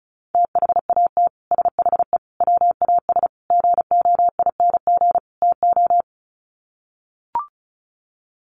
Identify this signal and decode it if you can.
Morse code, heard as a keyed tone, about 34 wpm, 700 Hz. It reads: THAT SHE WAS GOING TO
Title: Sets of 5 Words Encore2 35wpm